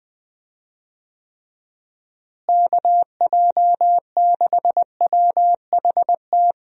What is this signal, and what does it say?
Morse code, heard as a keyed tone, about 20 wpm, 700 Hz. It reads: KJ6WHT
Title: USA Call Signs - 6 Characters - Repeated 3x - 20wpm